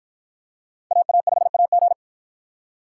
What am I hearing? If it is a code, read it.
Morse code, heard as a keyed tone, about 48 wpm, 700 Hz. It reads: AA5AG